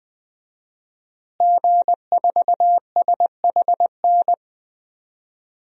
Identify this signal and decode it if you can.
Morse code, heard as a keyed tone, about 20 wpm, 700 Hz. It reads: G4SHN
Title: DXCC-100 Call Signs - 5 Characters - Repeated 2x - 20wpm